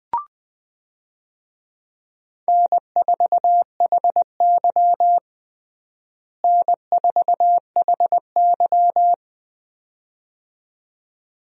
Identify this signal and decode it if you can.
Morse code, heard as a keyed tone, about 21 wpm, 700 Hz. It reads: N4HY N4HY
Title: USA Call Signs - 4 Characters - Repeated 2x - 20wpm